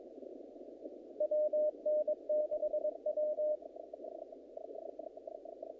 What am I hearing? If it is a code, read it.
Morse code, heard as a keyed tone, about 22 wpm, 610 Hz. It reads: WN6W
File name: contests / naqp-cw-k5tr-op-k5pi- -2000